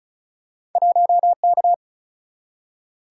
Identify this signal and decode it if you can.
Morse code, heard as a keyed tone, about 36 wpm, 700 Hz. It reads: 1K
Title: Single-Letter Single-Number 35wpm